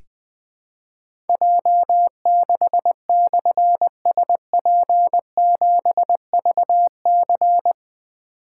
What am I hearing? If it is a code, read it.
Morse code, heard as a keyed tone, about 21 wpm, 700 Hz. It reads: J6/SP7VC
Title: Difficult Call Signs 20wpm